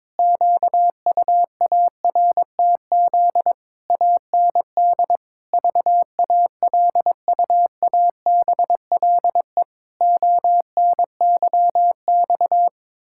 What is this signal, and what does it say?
Morse code, heard as a keyed tone, about 22 wpm, 700 Hz. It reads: QUARTZ AND VALUABLE ONYX
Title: Pangram Challenge v2 - 22wpm